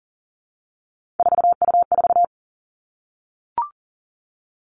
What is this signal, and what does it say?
Morse code, heard as a keyed tone, about 38 wpm, 700 Hz. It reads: 4U4
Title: Three Challenging Letter-Number Combinations 40wpm